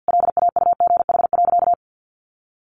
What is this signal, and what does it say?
Morse code, heard as a keyed tone, about 50 wpm, 700 Hz. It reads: LAUGH.
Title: Sentences from Top 1500 Words 50wpm